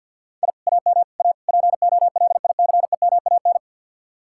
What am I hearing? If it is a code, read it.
Morse code, heard as a keyed tone, about 65 wpm, 700 Hz. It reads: I AM A POLICEMAN